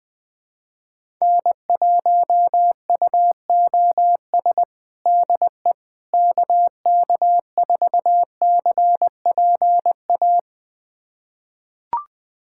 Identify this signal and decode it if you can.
Morse code, heard as a keyed tone, about 19 wpm, 700 Hz. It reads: N1UOS DE KK4CPA
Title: USA Callsign DE Callsign 20wpm